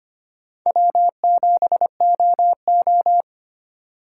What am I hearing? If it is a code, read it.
Morse code, heard as a keyed tone, about 34 wpm, 700 Hz. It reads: W7OO